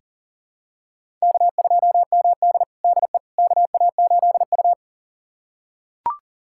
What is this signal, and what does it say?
Morse code, heard as a keyed tone, about 37 wpm, 700 Hz. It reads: K2MD DE KA8U
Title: USA Callsign DE Callsign - 4 Characters - 40wpm